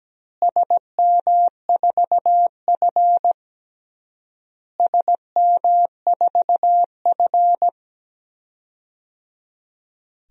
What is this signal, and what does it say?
Morse code, heard as a keyed tone, about 17 wpm, 700 Hz. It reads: SM4F SM4F
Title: DXCC-100 Call Signs - 4 Characters - Repeated 3x - 17wpm